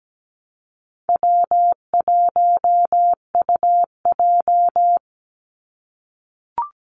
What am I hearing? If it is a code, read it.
Morse code, heard as a keyed tone, about 17 wpm, 700 Hz. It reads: W1UJ
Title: Easy Call Signs - 17wpm